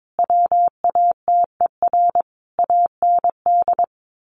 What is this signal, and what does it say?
Morse code, heard as a keyed tone, about 22 wpm, 700 Hz. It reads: WATER AND
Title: Sentences from Top 1300 Words 22wpm